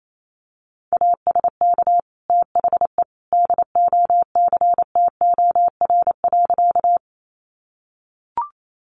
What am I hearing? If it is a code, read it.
Morse code, heard as a keyed tone, about 28 wpm, 700 Hz. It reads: ASK THE DOCTOR.
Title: Sentences from Top 1500 Words 28wpm